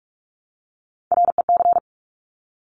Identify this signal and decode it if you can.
Morse code, heard as a keyed tone, about 46 wpm, 700 Hz. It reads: REC
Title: Top Three Letter Combinations in Words 45wpm